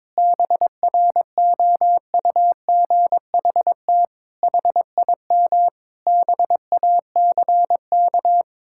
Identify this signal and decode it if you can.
Morse code, heard as a keyed tone, about 22 wpm, 700 Hz. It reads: BROUGHT HIM BACK
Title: Sentences from Top 1000 Words - Review Part 6 - 22wpm